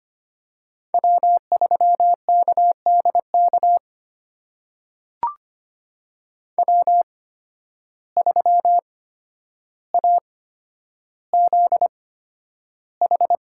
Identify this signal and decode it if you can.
Morse code, heard as a keyed tone, about 25 wpm, spaced as for 11 wpm, 700 Hz. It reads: W3KDK W3AZH
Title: USA Call Signs - 5 Characters - 8x Character Spacing - 25wpm